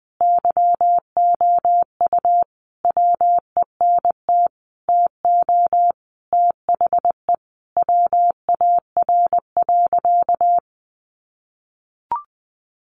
Morse code, heard as a keyed tone, about 20 wpm, 700 Hz. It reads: YOU WENT TO THE WAR.
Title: Sentences from Top 400 Words 20wpm